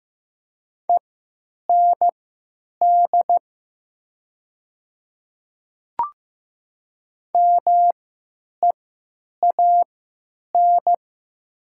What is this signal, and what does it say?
Morse code, heard as a keyed tone, about 15 wpm, spaced as for 9 wpm, 700 Hz. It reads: END MEAN
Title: 3x Character Spacing - Top 200 Words - 15wpm